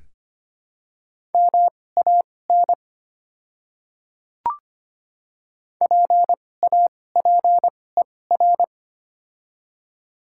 Morse code, heard as a keyed tone, about 25 wpm, spaced as for 18 wpm, 700 Hz. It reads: MAN PAPER
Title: 2x Character Spacing - Top 300 Words - 25wpm